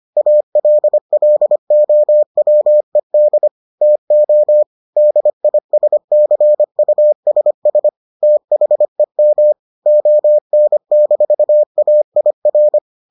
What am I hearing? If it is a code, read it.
Morse code, heard as a keyed tone, about 25 wpm, 600 Hz. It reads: ALLOWED TO DISCUSS THEM ON-AIR